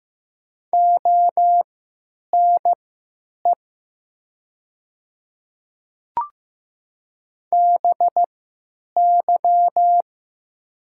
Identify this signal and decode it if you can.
Morse code, heard as a keyed tone, about 15 wpm, spaced as for 9 wpm, 700 Hz. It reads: ONE BY
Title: 3x Character Spacing - Top 100 Words - 15wpm